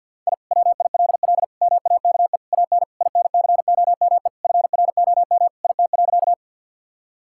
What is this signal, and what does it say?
Morse code, heard as a keyed tone, about 50 wpm, 700 Hz. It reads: I WILL MAKE AN INCOME FROM IT.
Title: Sentences from Top 1000 Words - Review Part 5 - 50wpm